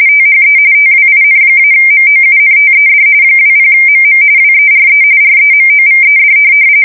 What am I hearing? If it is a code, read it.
RTTY, about 45.45 baud. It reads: THE TWIN-MOUNTED 20MM WERE REMOVED
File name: fsk170 / USS-New-Jersey